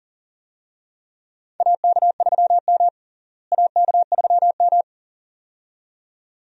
Morse code, heard as a keyed tone, about 38 wpm, 700 Hz. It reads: AK3M AK3M
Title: USA Call Signs - 4 Characters - Repeated 2x - 40wpm